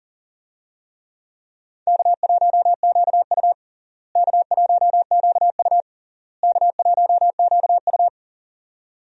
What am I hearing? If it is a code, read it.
Morse code, heard as a keyed tone, about 40 wpm, 700 Hz. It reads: K1QU K1QU K1QU
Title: USA Call Signs - 4 Characters - Repeated 3x - 40wpm